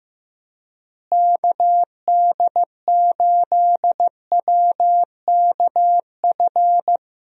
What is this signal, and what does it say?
Morse code, heard as a keyed tone, about 15 wpm, 700 Hz. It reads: KD8WKF
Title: USA Call Signs - 6 Characters - Repeated 3x - 15wpm